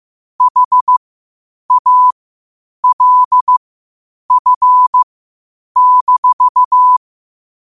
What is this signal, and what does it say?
Morse code, heard as a keyed tone, about 15 wpm, spaced as for 8 wpm, 1000 Hz. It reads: HALF-
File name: qotdfiles / 8wpm